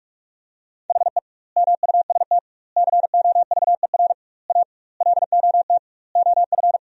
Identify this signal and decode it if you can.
Morse code, heard as a keyed tone, about 45 wpm, 700 Hz. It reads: HE MUST COVER A LOT OF
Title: 2x Word-Spacing - Sentences from Top 500 Words - 45wpm